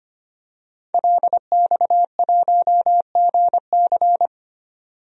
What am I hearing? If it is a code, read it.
Morse code, heard as a keyed tone, about 25 wpm, 700 Hz. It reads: LX1GC
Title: DXCC-100 Call Signs - 5 Characters - 25wpm